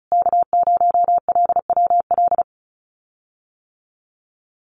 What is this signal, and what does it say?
Morse code, heard as a keyed tone, about 34 wpm, 700 Hz. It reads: K0LWL
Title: USA Call Signs Repeated 3x 35wpm